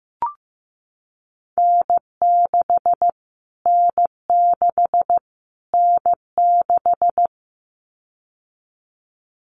Morse code, heard as a keyed tone, about 15 wpm, 700 Hz. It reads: N6 N6 N6
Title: Callsign Prefixes for Top 25 DXCC - Repeated 3x - 15wpm